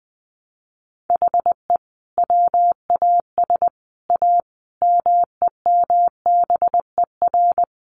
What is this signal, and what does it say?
Morse code, heard as a keyed tone, about 20 wpm, 700 Hz. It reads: HE WAS A MEMBER
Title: Sets of 5 Words Encore2 20wpm